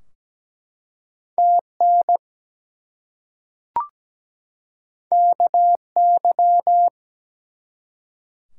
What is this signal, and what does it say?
Morse code, heard as a keyed tone, about 17 wpm, 700 Hz. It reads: TN KY
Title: US States and Canadian Provinces 17wpm